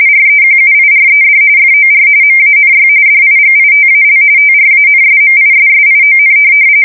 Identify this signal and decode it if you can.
RTTY, about 45.45 baud. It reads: REQUIRED FOR MANY DIGITAL TRANSMISSION
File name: fsk170 / teletype-RTTY